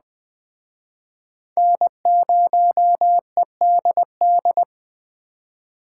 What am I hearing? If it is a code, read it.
Morse code, heard as a keyed tone, about 21 wpm, 700 Hz. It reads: N0EDD
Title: Speed Racing - USA 1x3 Call Signs 17wpm